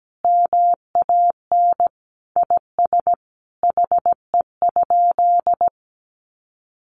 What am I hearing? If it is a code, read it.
Morse code, heard as a keyed tone, about 17 wpm, 700 Hz. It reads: MAN IS HE?
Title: Sentences from Top 1300 Words 17wpm